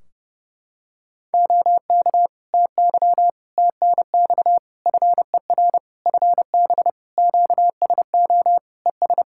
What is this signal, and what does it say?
Morse code, heard as a keyed tone, about 30 wpm, 700 Hz. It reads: OK TY TNX FER FB QSO ES
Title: K1USN SST QSO Practice 30wpm